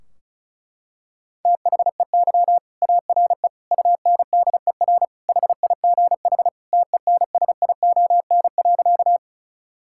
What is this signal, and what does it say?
Morse code, heard as a keyed tone, about 36 wpm, 700 Hz. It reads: THEY ARE UNDER HIGH TENSION.
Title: Sentences from Top 1600 Words 35wpm